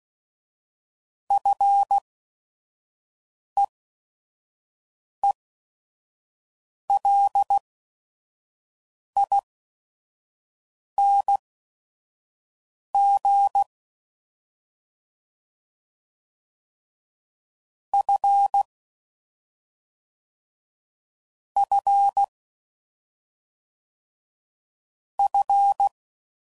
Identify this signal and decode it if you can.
Morse code, heard as a keyed tone, about 16 wpm, spaced as for 5 wpm, 800 Hz. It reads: FEELING F F F